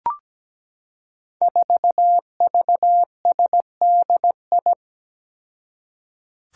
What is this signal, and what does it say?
Morse code, heard as a keyed tone, about 17 wpm, 700 Hz. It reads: 4VSDI
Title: Five Challenging Letter-Number Combinations 17wpm